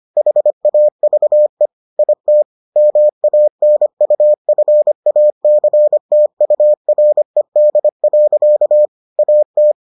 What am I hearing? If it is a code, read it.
Morse code, heard as a keyed tone, about 25 wpm, 600 Hz. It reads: HAVE IT MANUFACTURED. AT